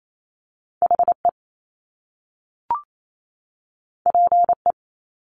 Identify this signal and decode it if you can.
Morse code, heard as a keyed tone, about 28 wpm, 700 Hz. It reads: HE PE
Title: Top Two Letter Combinations in Words 28wpm